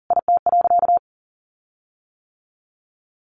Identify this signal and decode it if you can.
Morse code, heard as a keyed tone, about 37 wpm, 700 Hz. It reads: IT.
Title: Sentences from Top 1400 Words 40wpm